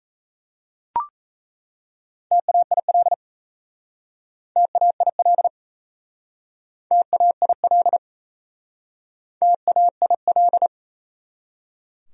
Morse code, heard as a keyed tone, about 37 wpm, 700 Hz. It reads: TAIL TAIL TAIL TAIL